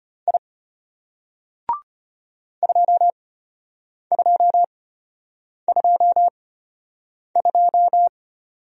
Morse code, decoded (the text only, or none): I 2222